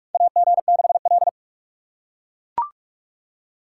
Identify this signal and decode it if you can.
Morse code, heard as a keyed tone, about 43 wpm, 700 Hz. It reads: AG6L
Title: USA Call Signs - 3x Character Spacing - 45wpm